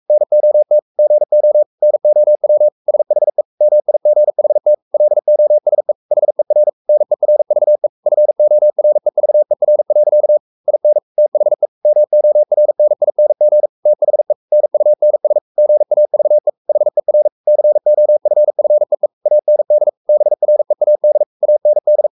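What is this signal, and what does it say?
Morse code, heard as a keyed tone, about 44 wpm, 600 Hz. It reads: NOT GO NOW SHE MIGHT LOSE HER NERVE FOREVER. IN THE MORNING THE NUNS GAVE HER COFFEE AND BREAD AND
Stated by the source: Morse code at 43 wpm sending Ken Follett's novel Never in English, from CW med Gnister.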